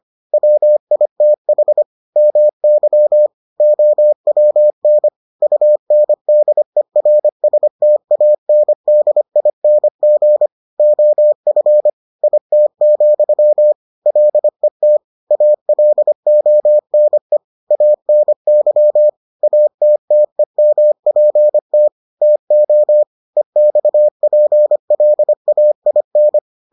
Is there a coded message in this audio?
Yes: Morse code from 0.3 to 26.4 s, heard as a keyed tone, about 25 wpm, 600 Hz. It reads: WITH MY OWN UNDERSTANDING OF IT, LET ALONE ANY ATTEMPT TO EXPLAIN